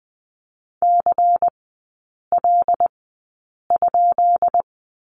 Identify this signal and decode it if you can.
Morse code, heard as a keyed tone, about 20 wpm, spaced as for 8 wpm, 690 Hz. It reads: CL?
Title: 191 Lesson 31e - Letter v - Three Characters 20wpm